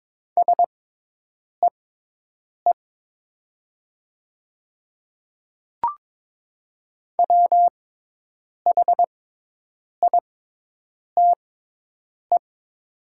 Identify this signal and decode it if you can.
Morse code, heard as a keyed tone, about 22 wpm, spaced as for 8 wpm, 700 Hz. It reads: SEE WHITE
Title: 6x Character Spacing - Top 300 Words - 22wpm